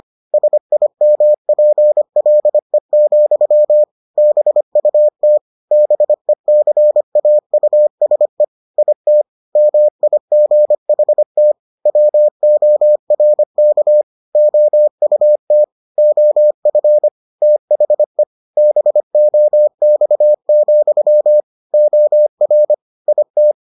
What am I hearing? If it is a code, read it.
Morse code, heard as a keyed tone, about 25 wpm, 600 Hz. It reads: SIMPLE, BUT BECAUSE IT MIGHT WORK OUT OF THE BOX, OR IT